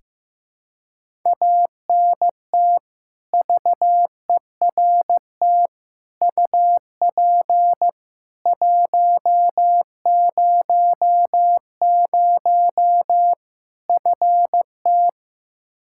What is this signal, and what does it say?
Morse code, heard as a keyed tone, about 15 wpm, 700 Hz. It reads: ANT VERT UP 100 FT
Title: Practice QSO - Exchange 2 - Info - 15wpm